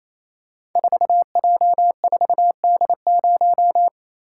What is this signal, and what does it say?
Morse code, heard as a keyed tone, about 27 wpm, 700 Hz. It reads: 4J4D0
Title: Code Groups - 5 Letter:Numbers - 28wpm